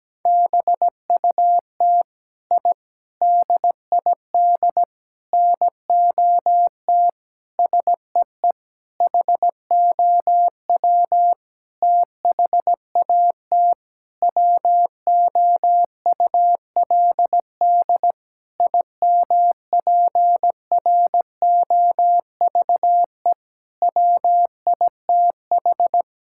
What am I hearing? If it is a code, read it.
Morse code, heard as a keyed tone, about 17 wpm, 700 Hz. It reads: BUT I DID NOT SEE HOW THAT WOULD IMPROVE WITH
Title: Sentences from Top 1000 Words - Review Part 5 - 17wpm